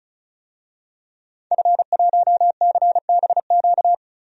Morse code, heard as a keyed tone, about 36 wpm, 700 Hz. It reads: F1CBQ
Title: DXCC-100 Call Signs - 5 Characters - 35wpm